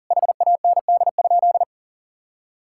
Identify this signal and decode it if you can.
Morse code, heard as a keyed tone, about 40 wpm, 700 Hz. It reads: HAND?
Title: Sentences from Top 1000 Words - Review Part 5 - 40wpm